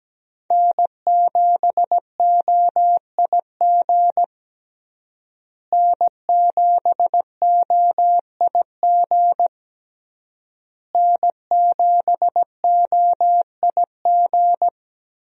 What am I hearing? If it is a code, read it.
Morse code, heard as a keyed tone, about 17 wpm, 700 Hz. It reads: N7OIG N7OIG N7OIG
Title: USA Call Signs - 5 Characters - Repeated 3x - 17wpm